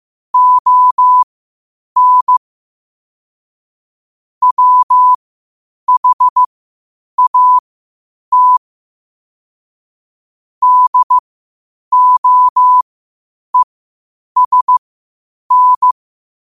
Morse code, heard as a keyed tone, about 15 wpm, spaced as for 8 wpm, 1000 Hz. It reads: ON WHAT DOESN
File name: 8wpm